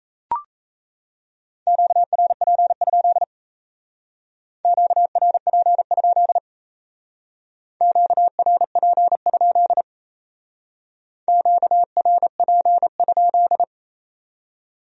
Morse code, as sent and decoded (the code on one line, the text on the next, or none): --.- .-. .--. ..--.. / --.- .-. .--. ..--.. / --.- .-. .--. ..--.. / --.- .-. .--. ..--..
QRP? QRP? QRP? QRP?